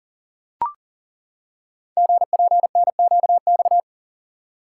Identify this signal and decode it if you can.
Morse code, heard as a keyed tone, about 42 wpm, 700 Hz. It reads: GPNQX
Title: Code Groups - 5 Letter:Numbers - 40wpm